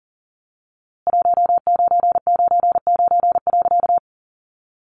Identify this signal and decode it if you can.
Morse code, heard as a keyed tone, about 40 wpm, 700 Hz. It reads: 1999.
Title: A Sound Of Thunder 40wpm